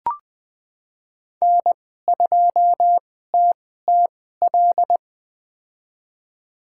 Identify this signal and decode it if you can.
Morse code, heard as a keyed tone, about 20 wpm, spaced as for 14 wpm, 700 Hz. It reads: N2TTL
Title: USA Call Signs - 5 Characters - 2x Character Spacing - 20wpm